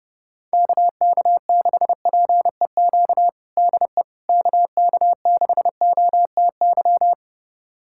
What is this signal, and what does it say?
Morse code, heard as a keyed tone, about 31 wpm, 700 Hz. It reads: KK6PEQ DE KK6OTY
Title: USA Callsign DE Callsign - 6 Characters - 30wpm